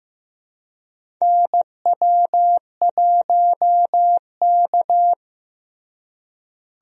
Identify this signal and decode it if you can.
Morse code, heard as a keyed tone, about 16 wpm, 700 Hz. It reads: NW1K
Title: USA Call Signs - 4 Characters - Repeated 3x - 15wpm